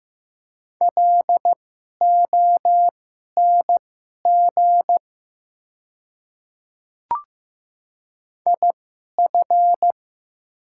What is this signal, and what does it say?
Morse code, heard as a keyed tone, about 15 wpm, spaced as for 11 wpm, 690 Hz. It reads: LONG IF
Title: 2x Character Spacing - Top 100 Words - 15wpm